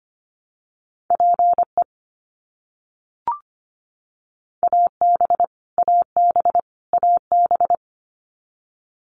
Morse code, heard as a keyed tone, about 25 wpm, 700 Hz. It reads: PE AB AB AB
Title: Canadian Provinces and Territories - Repeated 3x - 25wpm